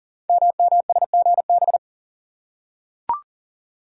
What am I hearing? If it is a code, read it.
Morse code, heard as a keyed tone, about 42 wpm, 690 Hz. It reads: MMSGB
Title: Code Groups - 5 Letters - 40wpm